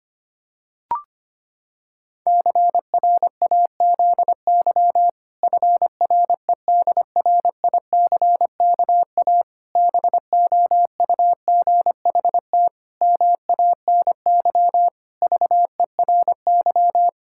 Morse code, heard as a keyed tone, about 25 wpm, 700 Hz. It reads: CRAZY FREDRICKA BOUGHT MANY VERY
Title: Pangram Challenge v2 - 25wpm